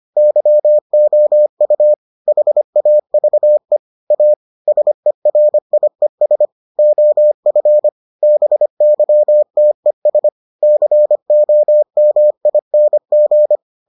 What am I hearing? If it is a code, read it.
Morse code, heard as a keyed tone, about 25 wpm, 600 Hz. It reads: YOU HAVE A SERIES OF BYTES COMING